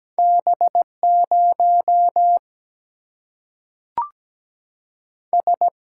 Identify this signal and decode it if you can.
Morse code, heard as a keyed tone, about 17 wpm, 700 Hz. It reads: B0 S
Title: Single-Letter Single-Number 17wpm